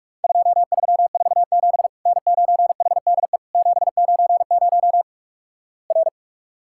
Morse code, heard as a keyed tone, about 48 wpm, 700 Hz. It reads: 2347 N9HDE 790 R
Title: Worked All Europe DX Contest - QTC Exchange - 45wpm